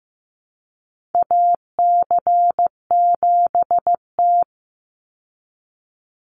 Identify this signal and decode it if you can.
Morse code, heard as a keyed tone, about 15 wpm, 700 Hz. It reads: AC7T
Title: Call Signs Repeated 2x - 15wpm